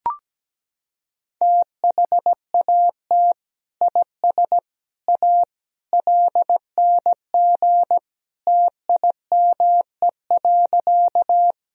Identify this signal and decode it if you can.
Morse code, heard as a keyed tone, about 17 wpm, 700 Hz. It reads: THAT IS A LNG TIME.
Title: Sentences from Top 100 Words and Abbreviations 17wpm